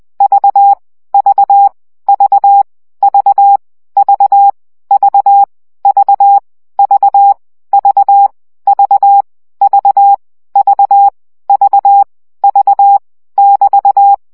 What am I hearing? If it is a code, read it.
Morse code, heard as a keyed tone, about 20 wpm, 800 Hz. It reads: V V V V V V V V V V V V V V <BT>
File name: CW gyakorlas v 17 baud